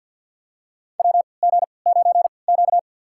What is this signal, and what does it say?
Morse code, heard as a keyed tone, about 47 wpm, spaced as for 28 wpm, 700 Hz. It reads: WG9Q
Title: USA Call Signs - 4 Characters - 3x Character Spacing - 50wpm